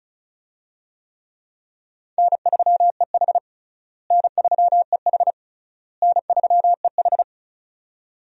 Morse code, heard as a keyed tone, about 36 wpm, 700 Hz. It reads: N3EH N3EH N3EH
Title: USA Call Signs - 4 Characters - Repeated 3x - 35wpm